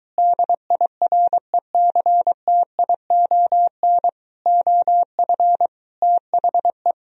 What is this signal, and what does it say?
Morse code, heard as a keyed tone, about 23 wpm, 700 Hz. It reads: DIRECTION OF THE